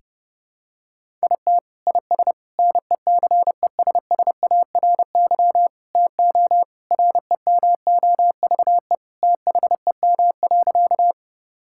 Morse code, heard as a keyed tone, about 30 wpm, 700 Hz. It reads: IT IS NECESSARY TO REMOVE THEM.
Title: Sentences from Top 1000 Words - Review Part 5 - 30wpm